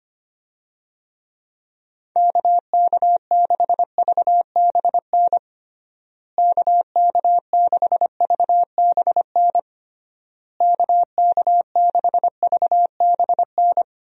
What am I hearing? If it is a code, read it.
Morse code, heard as a keyed tone, about 25 wpm, 700 Hz. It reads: KK6VBN KK6VBN KK6VBN
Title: USA Call Signs - 6 Characters - Repeated 3x - 25wpm